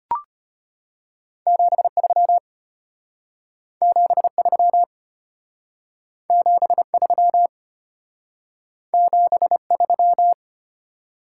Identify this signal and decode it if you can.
Morse code, heard as a keyed tone, about 33 wpm, 700 Hz. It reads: 73 73 73 73